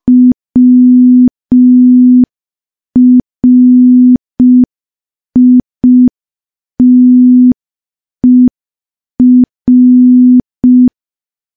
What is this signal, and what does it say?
Morse code, heard as a keyed tone, about 5 wpm, 250 Hz. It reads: WRITER
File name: morse / morse-training-05WPM